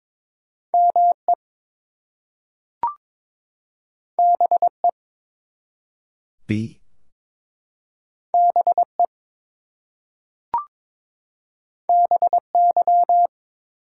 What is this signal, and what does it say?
Morse code, heard as a keyed tone, about 22 wpm, 700 Hz. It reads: ME BE BE BY
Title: Two Letter Words - 22wpm